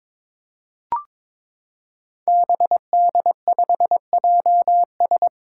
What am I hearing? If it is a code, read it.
Morse code, heard as a keyed tone, about 22 wpm, 700 Hz. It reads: BD5JS
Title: Five Challenging Letter-Number Combinations 22wpm